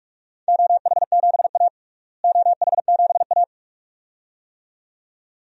Morse code, heard as a keyed tone, about 44 wpm, 700 Hz. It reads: OH7A OH7A
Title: DXCC-100 Call Signs - 4 Characters - Repeated 2x - 45wpm